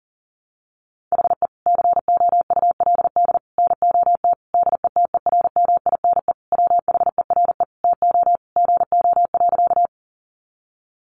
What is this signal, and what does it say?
Morse code, heard as a keyed tone, about 40 wpm, 700 Hz. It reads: HE COULD NOT DETERMINE WHERE TO GO.
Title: Sentences from Top 700 Words 40wpm